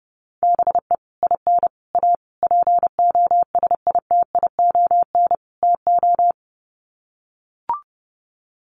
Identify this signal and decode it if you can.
Morse code, heard as a keyed tone, about 30 wpm, 700 Hz. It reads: BE IN A POSITION TO
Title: Sets of 5 Words Encore2 30wpm